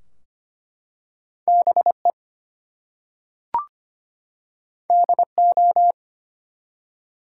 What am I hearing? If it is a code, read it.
Morse code, heard as a keyed tone, about 25 wpm, 700 Hz. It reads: BE DO
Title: Two Letter Words - 25wpm